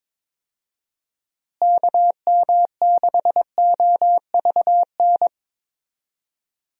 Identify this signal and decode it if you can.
Morse code, heard as a keyed tone, about 22 wpm, 700 Hz. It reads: KM6OVN
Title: USA Call Signs - 6 Characters - Repeated 2x - 22wpm